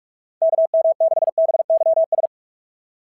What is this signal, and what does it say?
Morse code, heard as a keyed tone, about 48 wpm, 660 Hz. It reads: KM6BYS
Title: SOTA QSO Practice 45wpm